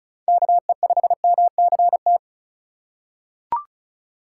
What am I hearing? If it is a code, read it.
Morse code, heard as a keyed tone, about 36 wpm, 700 Hz. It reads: KE5MCT
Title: USA Call Signs - 6 Characters - 6x Character Spacing - 35wpm